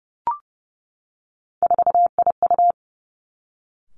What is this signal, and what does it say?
Morse code, heard as a keyed tone, about 30 wpm, 700 Hz. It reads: 4IU
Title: Three Challenging Letter-Number Combinations 30wpm